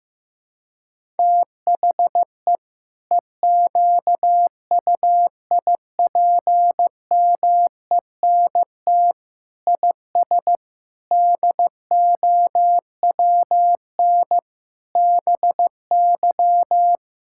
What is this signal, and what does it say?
Morse code, heard as a keyed tone, about 16 wpm, 700 Hz. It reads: THE EQUIPMENT IS DOWN BY
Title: Sentences from Top 1000 Words - Review Part 5 - 15wpm